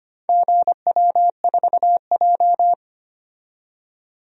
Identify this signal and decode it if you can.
Morse code, heard as a keyed tone, about 25 wpm, 700 Hz. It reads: GW4J
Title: DXCC-100 Call Signs - 4 Characters - 25wpm